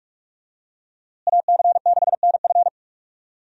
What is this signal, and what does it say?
Morse code, heard as a keyed tone, about 46 wpm, 700 Hz. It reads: AC6NF
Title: USA Call Signs - 5 Characters - Repeated 2x - 45wpm